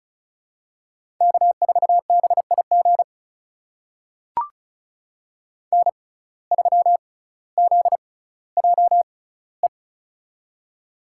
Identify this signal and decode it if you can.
Morse code, heard as a keyed tone, about 36 wpm, spaced as for 18 wpm, 700 Hz. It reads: K4BIG N3ZJE
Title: USA Call Signs - 5 Characters - 6x Character Spacing - 35wpm